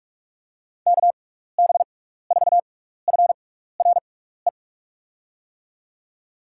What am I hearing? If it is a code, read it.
Morse code, heard as a keyed tone, about 44 wpm, spaced as for 15 wpm, 700 Hz. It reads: KB4FRE
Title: USA Call Signs - 6 Characters - 6x Character Spacing - 45wpm